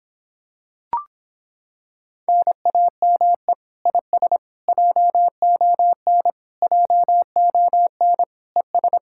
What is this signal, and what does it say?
Morse code, heard as a keyed tone, about 26 wpm, 700 Hz. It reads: NAME IS JON JON ES